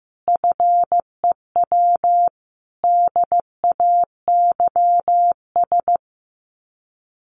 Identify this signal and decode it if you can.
Morse code, heard as a keyed tone, about 15 wpm, 700 Hz. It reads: FEW DAYS
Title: Sets of 5 Words Encore2 15wpm